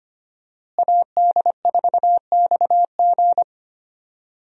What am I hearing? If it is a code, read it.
Morse code, heard as a keyed tone, about 25 wpm, 700 Hz. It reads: AD4XG
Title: USA Call Signs - 5 Characters - Repeated 2x - 25wpm